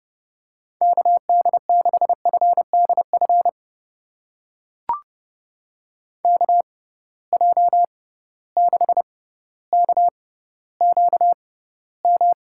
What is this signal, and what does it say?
Morse code, heard as a keyed tone, about 30 wpm, spaced as for 15 wpm, 700 Hz. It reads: KD6FDF KJ6KQM
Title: USA Call Signs - 6 Characters - 6x Character Spacing - 30wpm